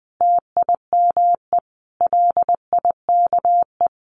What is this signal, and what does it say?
Morse code, heard as a keyed tone, about 20 wpm, 700 Hz. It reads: TIME LIKE
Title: Sentences from Top 600 Words 20wpm